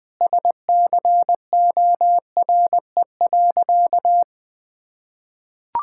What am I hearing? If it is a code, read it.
Morse code, heard as a keyed tone, about 20 wpm, 700 Hz. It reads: SCORE.
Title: Sentences from Top 1600 Words 20wpm